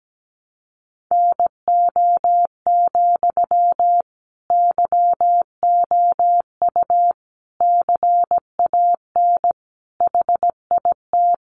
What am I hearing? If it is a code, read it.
Morse code, heard as a keyed tone, about 17 wpm, 700 Hz. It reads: NO, YOU CAN HIT
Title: Sentences from Top 1200 Words 17wpm